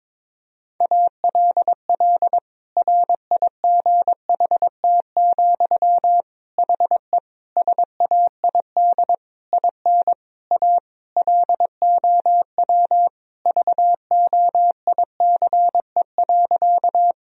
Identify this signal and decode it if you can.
Morse code, heard as a keyed tone, about 21 wpm, 700 Hz. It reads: ALL RIGHT, HE SAID IN A LOW VOICE.
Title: Sentences from Top 1000 Words - Review Part 1 - 22wpm